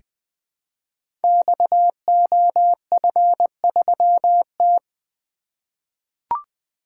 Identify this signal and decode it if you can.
Morse code, heard as a keyed tone, about 20 wpm, 700 Hz. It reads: XOF3T
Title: Code Groups - 5 Letter:Numbers - 20wpm